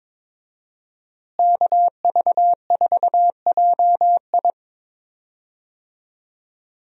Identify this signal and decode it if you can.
Morse code, heard as a keyed tone, about 22 wpm, 700 Hz. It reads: KV4JI
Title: USA Call Signs - 5 Characters - Repeated 3x - 22wpm